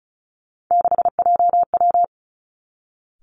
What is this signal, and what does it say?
Morse code, heard as a keyed tone, about 36 wpm, 700 Hz. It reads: 6JW
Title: Three Challenging Letter-Number Combinations 35wpm